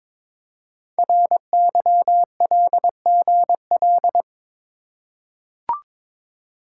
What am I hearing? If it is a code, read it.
Morse code, heard as a keyed tone, about 22 wpm, 700 Hz. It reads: RYLGL
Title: Code Groups - 5 Letters - 22wpm